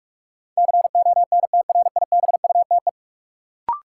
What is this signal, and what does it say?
Morse code, heard as a keyed tone, about 46 wpm, 700 Hz. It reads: CONTRIBUTE
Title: Sentences from Top 1400 Words - New Word Drill Alt - 45wpm